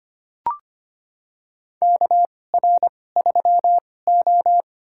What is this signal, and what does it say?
Morse code, heard as a keyed tone, about 25 wpm, spaced as for 18 wpm, 700 Hz. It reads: KR3O
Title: USA Call Signs - 4 Characters - 2x Character Spacing - 25wpm